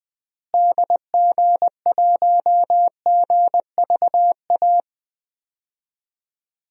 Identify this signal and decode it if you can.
Morse code, heard as a keyed tone, about 20 wpm, 700 Hz. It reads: DG1GVA
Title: DXCC-100 Call Signs - 6 Characters - Repeated 2x - 20wpm